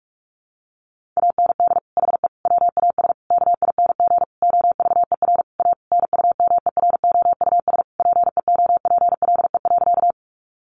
Morse code, heard as a keyed tone, about 44 wpm, 700 Hz. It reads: AND HE WAS KING OVER A NUMEROUS PEOPLE.
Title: Sentences from Top 1400 Words 45wpm